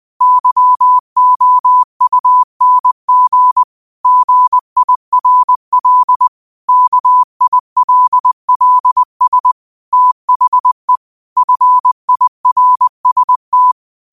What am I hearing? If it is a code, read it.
Morse code, heard as a keyed tone, about 20 wpm, 1000 Hz. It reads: YOUNG GIRL KILLS THE FIRST
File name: qotdfiles / 20wpm